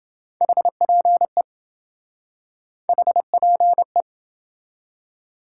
Morse code, heard as a keyed tone, about 27 wpm, 700 Hz. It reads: HPE HPE